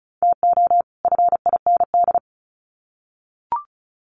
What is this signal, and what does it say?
Morse code, heard as a keyed tone, about 34 wpm, 700 Hz. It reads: TO FIND
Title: Sets of 2 Words 35wpm